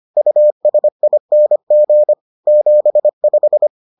Morse code, heard as a keyed tone, about 25 wpm, 600 Hz. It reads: USING 75